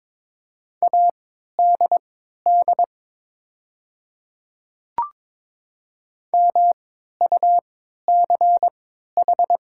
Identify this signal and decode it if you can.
Morse code, heard as a keyed tone, about 22 wpm, spaced as for 12 wpm, 700 Hz. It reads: ADD MUCH
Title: 3x Character Spacing - Top 300 Words - 22wpm